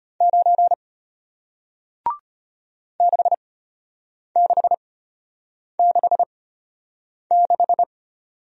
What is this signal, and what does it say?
Morse code, heard as a keyed tone, about 34 wpm, spaced as for 8 wpm, 700 Hz. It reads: 9 6666